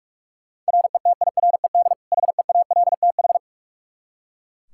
Morse code, heard as a keyed tone, about 42 wpm, 700 Hz. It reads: RETIRED HEALTH
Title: Practice QSO - Exchange 3 - More Info - 45wpm